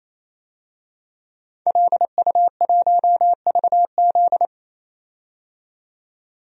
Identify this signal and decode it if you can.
Morse code, heard as a keyed tone, about 28 wpm, 700 Hz. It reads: LU1VZ
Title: DXCC-100 Call Signs - 5 Characters - 28wpm